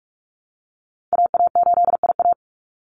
Morse code, heard as a keyed tone, about 44 wpm, 700 Hz. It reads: AA8IA
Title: Call Signs Repeated 2x - 45wpm